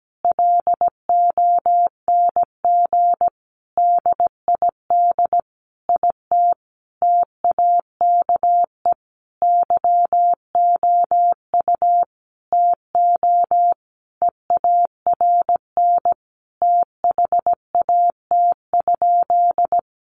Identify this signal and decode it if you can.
Morse code, heard as a keyed tone, about 17 wpm, 700 Hz. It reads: LONG DID IT TAKE YOU TO EARN THAT?
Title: Sentences from Top 1500 Words 17wpm